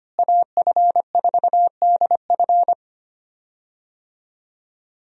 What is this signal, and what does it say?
Morse code, heard as a keyed tone, about 25 wpm, 700 Hz. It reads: AF4DF
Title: USA Call Signs - 5 Characters - Repeated 2x - 25wpm